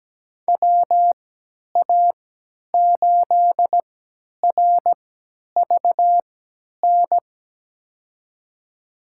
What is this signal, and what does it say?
Morse code, heard as a keyed tone, about 17 wpm, spaced as for 10 wpm, 700 Hz. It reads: WA8RVN
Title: USA Call Signs - 6 Characters - 3x Character Spacing - 17wpm